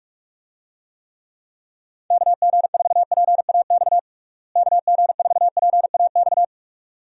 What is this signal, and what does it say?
Morse code, heard as a keyed tone, about 47 wpm, 700 Hz. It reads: KG4PAX KG4PAX
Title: USA Call Signs - 6 Characters - Repeated 2x - 45wpm